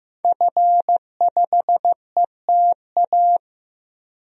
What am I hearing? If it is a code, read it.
Morse code, heard as a keyed tone, about 15 wpm, 700 Hz. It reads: F5ETA
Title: DXCC-100 Call Signs - 5 Characters - Repeated 2x - 15wpm